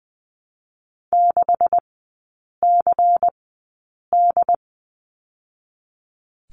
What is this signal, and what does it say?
Morse code, heard as a keyed tone, about 20 wpm, spaced as for 8 wpm, 700 Hz. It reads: 6CD
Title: 152 Lesson 25e - Letter p - Three Characters 20wpm